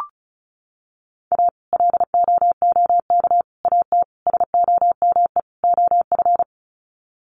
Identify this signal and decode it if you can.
Morse code, heard as a keyed tone, about 34 wpm, 700 Hz. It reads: A LOOK AT SOME OF
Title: Sets of 5 Words Encore2 35wpm